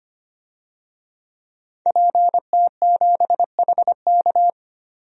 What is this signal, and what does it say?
Morse code, heard as a keyed tone, about 25 wpm, 700 Hz. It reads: PT7HK
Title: DXCC-100 Call Signs - 5 Characters - 25wpm